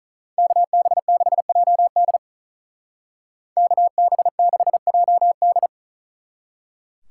Morse code, heard as a keyed tone, about 40 wpm, 700 Hz. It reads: KB6JD KB6JD
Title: Speed Racing - USA 2x2 Call Signs 35wpm